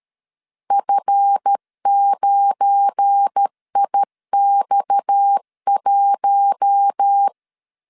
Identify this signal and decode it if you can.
Morse code, heard as a keyed tone, about 13 wpm, 800 Hz. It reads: F9IX1
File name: CW / DET / 1 9 1500C S B 12wpm 3819